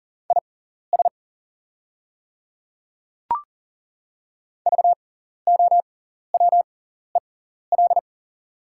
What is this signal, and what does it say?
Morse code, heard as a keyed tone, about 40 wpm, spaced as for 14 wpm, 690 Hz. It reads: IS VOWEL
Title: 6x Character Spacing - Top 500 Words - 40wpm